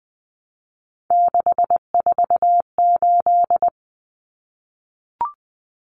Morse code, heard as a keyed tone, about 20 wpm, 700 Hz. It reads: 648
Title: Numbers - 3-Digit 20wpm